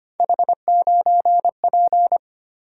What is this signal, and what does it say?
Morse code, heard as a keyed tone, about 25 wpm, 700 Hz. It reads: H9P
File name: Three Letter-Number Combinations - 25wpm